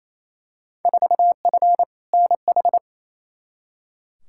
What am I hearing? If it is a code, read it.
Morse code, heard as a keyed tone, about 28 wpm, 700 Hz. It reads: 4F NH
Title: Field Day Exchanges Encore - 28wpm